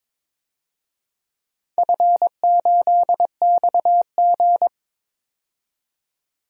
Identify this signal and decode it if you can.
Morse code, heard as a keyed tone, about 22 wpm, 700 Hz. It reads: F8XG
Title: DXCC-100 Call Signs - 4 Characters - 22wpm